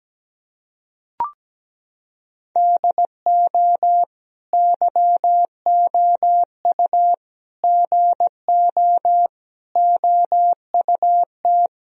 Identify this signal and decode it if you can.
Morse code, heard as a keyed tone, about 17 wpm, 700 Hz. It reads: DO YOU GO OUT
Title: Sentences from Top 1000 Words - Review Part 4 - 17wpm